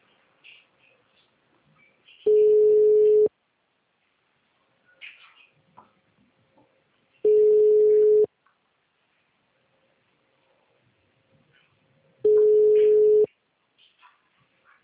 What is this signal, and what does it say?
Ringback tone (Europe, 425 Hz).